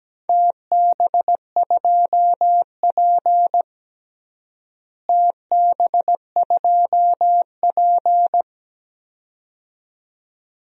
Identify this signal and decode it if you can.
Morse code, heard as a keyed tone, about 17 wpm, 700 Hz. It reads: TB2P TB2P
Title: DXCC-100 Call Signs - 4 Characters - Repeated 2x - 17wpm